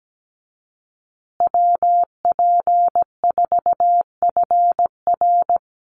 Morse code, heard as a keyed tone, about 17 wpm, 700 Hz. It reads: WP4FR
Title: USA Call Signs Repeated 2x 17wpm